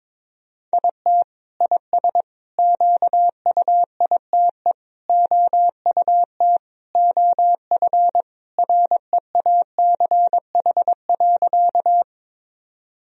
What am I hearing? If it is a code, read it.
Morse code, heard as a keyed tone, about 22 wpm, 700 Hz. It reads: IT IS QUITE OUT OF REACH.
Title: Sentences from Top 1000 Words - Review Part 2 - 22wpm